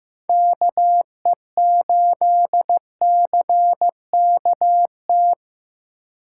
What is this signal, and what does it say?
Morse code, heard as a keyed tone, about 18 wpm, 700 Hz. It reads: KE8CKT
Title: Speed Racing - USA Call Signs 15wpm